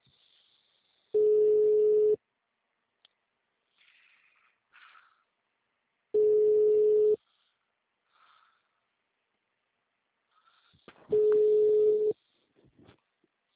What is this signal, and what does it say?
Ringback tone (Europe, 425 Hz).
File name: out- -2007- - - .225